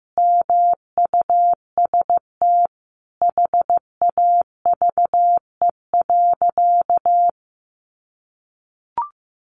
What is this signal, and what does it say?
Morse code, heard as a keyed tone, about 14 wpm, 700 Hz. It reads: MUST HAVE.
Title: Sentences from Top 1100 Words 15wpm